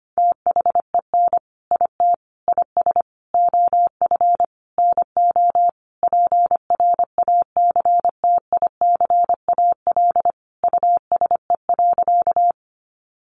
Morse code, heard as a keyed tone, about 25 wpm, 700 Hz. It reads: THEN IT IS OF NO PRACTICAL USE.
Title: Sentences from Top 1400 Words 25wpm